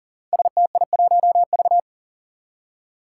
Morse code, heard as a keyed tone, about 42 wpm, 700 Hz. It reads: STI1V
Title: Five Challenging Letter-Number Combinations 40wpm